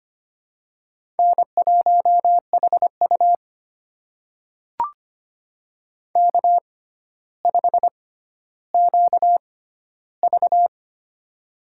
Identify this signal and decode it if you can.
Morse code, heard as a keyed tone, about 25 wpm, spaced as for 13 wpm, 700 Hz. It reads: N1HU K5QV
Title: USA Call Signs - 4 Characters - 6x Character Spacing - 25wpm